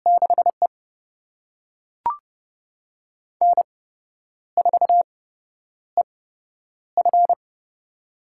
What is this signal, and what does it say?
Morse code, heard as a keyed tone, about 32 wpm, spaced as for 10 wpm, 700 Hz. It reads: 6E N4EF
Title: USA Call Signs - 4 Characters - 8x Character Spacing - 30wpm